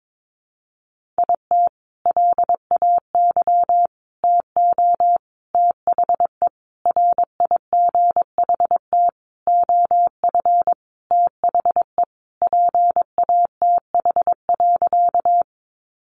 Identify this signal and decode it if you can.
Morse code, heard as a keyed tone, about 22 wpm, 700 Hz. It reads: IT LAY TO THE RIGHT OF THE PATH.
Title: Sentences from Top 1400 Words 22wpm